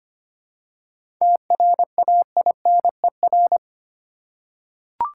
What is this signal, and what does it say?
Morse code, heard as a keyed tone, about 25 wpm, 700 Hz. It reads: TRAINER
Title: Seven Letter Words - 25wpm